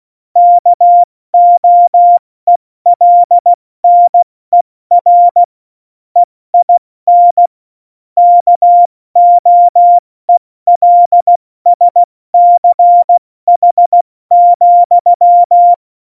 Morse code, heard as a keyed tone, about 16 wpm, 700 Hz. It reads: KOELNER EIN KOELSCH,
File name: Morsekurs / Witze / Witz 08. Oktober 2019, 70 F80